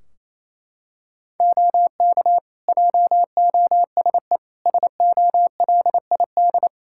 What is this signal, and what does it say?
Morse code, heard as a keyed tone, about 28 wpm, 700 Hz. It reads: OK JOSE SOLID
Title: K1USN SST QSO Practice 28wpm